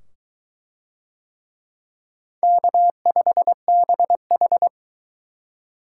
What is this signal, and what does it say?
Morse code, heard as a keyed tone, about 22 wpm, 700 Hz. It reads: K5BH